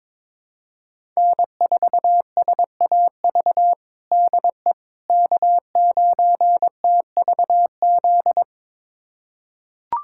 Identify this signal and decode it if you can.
Morse code, heard as a keyed tone, about 22 wpm, 700 Hz. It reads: N4SAV DE K9TVZ
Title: USA Callsign DE Callsign - 5 Characters - 22wpm